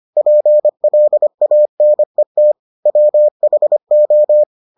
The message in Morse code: PLANET WHO

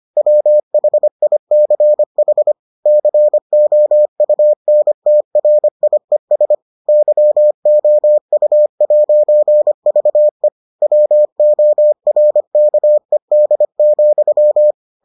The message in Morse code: WHICH COUNTRIES YOU'VE WORKED,

